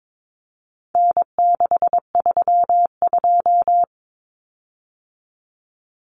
Morse code, heard as a keyed tone, about 22 wpm, 700 Hz. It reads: N632
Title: Common Cut Numbers - 4-Digit 22wpm